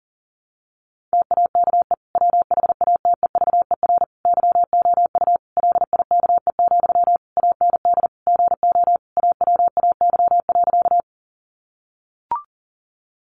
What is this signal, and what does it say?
Morse code, heard as a keyed tone, about 40 wpm, 700 Hz. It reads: TAKE WHATEVER YOU LIKE, AND GO AWAY.
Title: Sentences from Top 1100 Words 40wpm